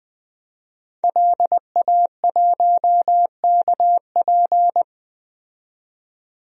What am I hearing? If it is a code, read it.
Morse code, heard as a keyed tone, about 20 wpm, 700 Hz. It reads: LA1KP
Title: DXCC-100 Call Signs - 5 Characters - 20wpm